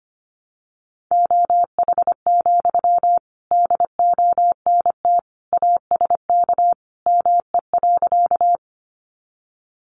Morse code, heard as a keyed tone, about 25 wpm, 700 Hz. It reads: OH, DONT ASK ME.
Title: Sentences from Top 500 Words 25wpm